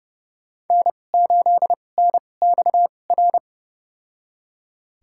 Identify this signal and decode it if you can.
Morse code, heard as a keyed tone, about 30 wpm, spaced as for 22 wpm, 700 Hz. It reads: N8NXR
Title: USA Call Signs - 5 Characters - 2x Character Spacing - 30wpm